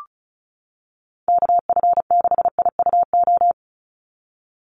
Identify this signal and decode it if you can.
Morse code, heard as a keyed tone, about 34 wpm, 700 Hz. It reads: KF6IUO
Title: USA Call Signs 35wpm